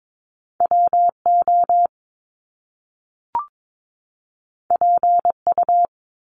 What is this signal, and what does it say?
Morse code, heard as a keyed tone, about 22 wpm, 700 Hz. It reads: WO PU
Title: Top Two Letter Combinations in Words 22wpm